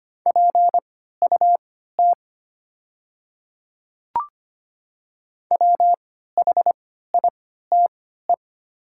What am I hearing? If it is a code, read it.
Morse code, heard as a keyed tone, about 25 wpm, spaced as for 14 wpm, 700 Hz. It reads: PUT WHITE
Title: 3x Character Spacing - Top 500 Words - 25wpm